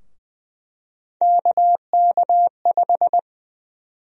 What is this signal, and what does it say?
Morse code, heard as a keyed tone, about 20 wpm, 700 Hz. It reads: KK5
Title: USA Call Sign Prefixes - 20wpm